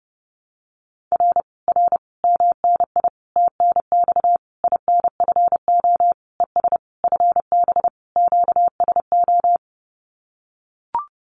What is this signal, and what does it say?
Morse code, heard as a keyed tone, about 30 wpm, 700 Hz. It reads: R R MNI TNX INFO ES FB QSO
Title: Common QSO Phrases Encore 30wpm